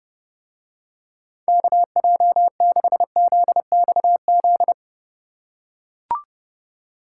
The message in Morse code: KJ6ZXZ

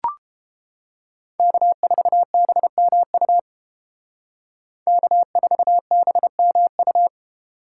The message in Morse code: K4BMU K4BMU

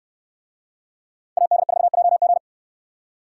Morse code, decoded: AD4JG

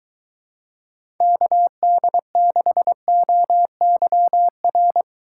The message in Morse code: KD6OYR

